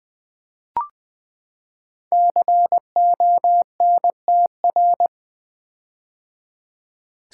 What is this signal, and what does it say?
Morse code, heard as a keyed tone, about 20 wpm, 700 Hz. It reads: CONTR
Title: Build A Word - 5 Letters At A Time - 12-Letter Words - 20wpm